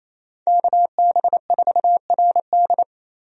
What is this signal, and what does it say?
Morse code, heard as a keyed tone, about 27 wpm, 700 Hz. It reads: KB4RD
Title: USA Call Signs - 5 Characters - Repeated 2x - 28wpm